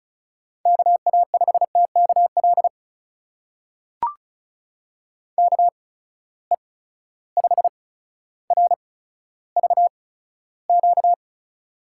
Morse code, heard as a keyed tone, about 36 wpm, spaced as for 15 wpm, 700 Hz. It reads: KA5TKL KE5RVQ
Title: USA Call Signs - 6 Characters - 8x Character Spacing - 35wpm